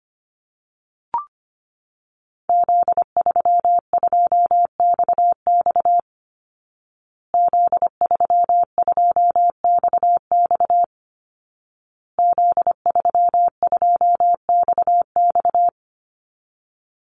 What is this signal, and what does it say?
Morse code, heard as a keyed tone, about 25 wpm, 700 Hz. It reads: Z32XX Z32XX Z32XX
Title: Call Signs Repeated 2x - 25wpm